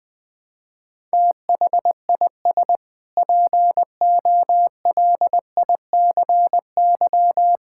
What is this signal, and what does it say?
Morse code, heard as a keyed tone, about 20 wpm, 700 Hz. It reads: THIS POLICY
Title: Sentences with 6-Word Phrases - 20wpm